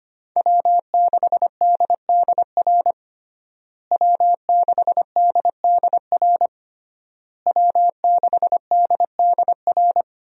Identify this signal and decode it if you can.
Morse code, heard as a keyed tone, about 25 wpm, 700 Hz. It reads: W6DDR W6DDR W6DDR
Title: USA Call Signs - 5 Characters - Repeated 3x - 25wpm